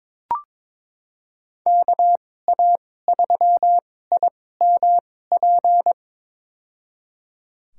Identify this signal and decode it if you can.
Morse code, heard as a keyed tone, about 22 wpm, spaced as for 16 wpm, 700 Hz. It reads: KA3IMP
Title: USA Call Signs - 6 Characters - 2x Character Spacing - 22wpm